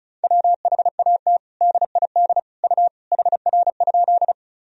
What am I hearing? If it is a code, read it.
Morse code, heard as a keyed tone, about 36 wpm, 700 Hz. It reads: WHAT DID U HR?
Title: Sentences from Top 300 Words and Abbreviations 35wpm